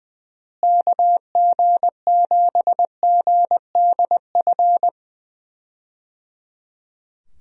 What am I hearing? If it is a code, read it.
Morse code, heard as a keyed tone, about 20 wpm, 700 Hz. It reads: KG7GDF
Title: USA Call Signs - 6 Characters - Repeated 3x - 20wpm